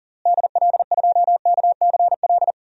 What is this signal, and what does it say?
Morse code, heard as a keyed tone, about 37 wpm, 700 Hz. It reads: DL2KCL
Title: DXCC-100 Call Signs - 6 Characters - Repeated 3x - 40wpm